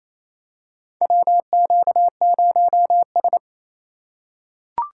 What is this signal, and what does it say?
Morse code, heard as a keyed tone, about 28 wpm, 700 Hz. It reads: WQ0S
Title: USA Call Signs - 4 Characters - 4x Character Spacing - 28wpm